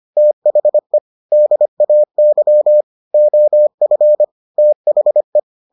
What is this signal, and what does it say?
Morse code, heard as a keyed tone, about 25 wpm, 600 Hz. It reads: THE DAY OF THE